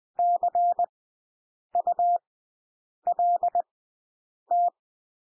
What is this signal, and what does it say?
Morse code, heard as a keyed tone, about 20 wpm, spaced as for 8 wpm, 700 Hz. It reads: CULT
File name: cwmp3 / W1 SOUND WORDS 20 wpm